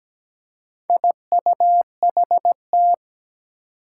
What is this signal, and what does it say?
Morse code, heard as a keyed tone, about 17 wpm, 700 Hz. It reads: IUHT
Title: Four Challenging Letter-Number Combinations 17wpm